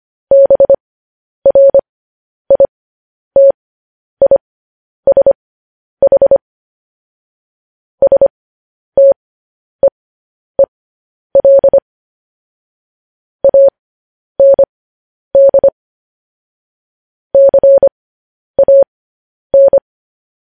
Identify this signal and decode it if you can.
Morse code, heard as a keyed tone, about 25 wpm, spaced as for 10 wpm, 550 Hz. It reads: BRITISH STEEL AND CAN
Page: news headlines in morse code at 10 wpm